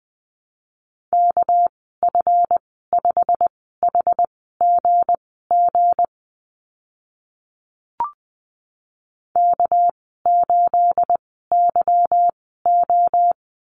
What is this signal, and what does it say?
Morse code, heard as a keyed tone, about 20 wpm, spaced as for 14 wpm, 700 Hz. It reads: KF5HGG K8YO
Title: 245 Lesson 39g - Letter z - Call Signs 20wpm